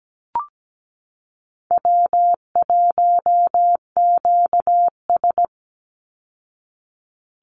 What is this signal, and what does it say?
Morse code, heard as a keyed tone, about 17 wpm, 700 Hz. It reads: W1QS
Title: Easy Call Signs Repeated 3x - 17wpm